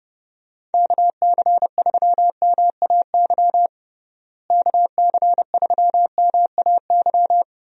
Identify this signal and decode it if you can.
Morse code, heard as a keyed tone, about 32 wpm, 700 Hz. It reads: KC3MAY KC3MAY
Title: USA Call Signs - 6 Characters - Repeated 3x - 30wpm